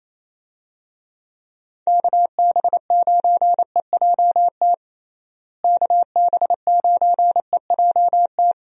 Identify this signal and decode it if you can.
Morse code, heard as a keyed tone, about 28 wpm, 700 Hz. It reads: KB9EJT KB9EJT
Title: USA Call Signs - 6 Characters - Repeated 3x - 28wpm